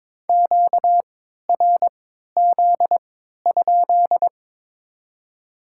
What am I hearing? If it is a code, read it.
Morse code, heard as a keyed tone, about 22 wpm, spaced as for 12 wpm, 700 Hz. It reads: QRZ?
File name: Top 90 QSO Elements - 3x Character Spacing - 22wpm